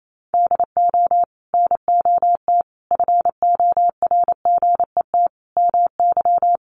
Morse code, heard as a keyed tone, about 28 wpm, 700 Hz. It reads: DO NOT FORGET MY
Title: Sentences from Top 1200 Words 28wpm